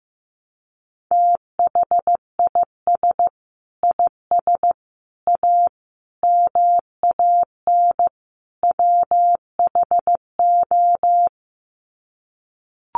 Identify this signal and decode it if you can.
Morse code, heard as a keyed tone, about 15 wpm, 700 Hz. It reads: THIS IS A MAN WHO
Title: Sets of 5 Words Encore2 15wpm